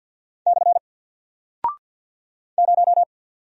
Morse code, heard as a keyed tone, about 48 wpm, 700 Hz. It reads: / 0
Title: Single Character (Letter:Number:Punctuation:Prosign) Alt 50wpm